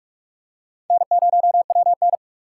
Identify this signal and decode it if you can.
Morse code, heard as a keyed tone, about 42 wpm, 700 Hz. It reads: N0WN
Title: USA Call Signs - 4 Characters - 4x Character Spacing - 45wpm